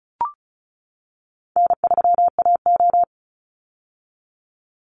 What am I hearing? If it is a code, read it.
Morse code, heard as a keyed tone, about 34 wpm, 700 Hz. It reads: N3AO
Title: Call Signs - 35wpm